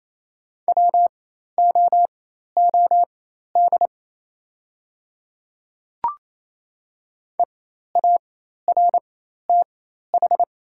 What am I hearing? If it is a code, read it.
Morse code, heard as a keyed tone, about 28 wpm, spaced as for 13 wpm, 700 Hz. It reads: WOOD EARTH
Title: 4x Character Spacing - Top 500 Words - 28wpm